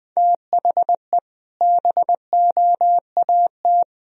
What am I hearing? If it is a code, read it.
Morse code, heard as a keyed tone, about 20 wpm, 700 Hz. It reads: THE BOAT
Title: Sentences from Top 1000 Words - Review Part 3 - 20wpm